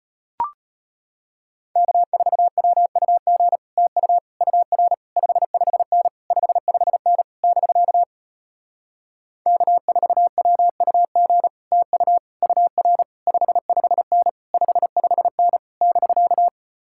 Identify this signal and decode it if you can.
Morse code, heard as a keyed tone, about 33 wpm, 700 Hz. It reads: K4WUG TU UR 55N 55N <BK> K4WUG TU UR 55N 55N <BK>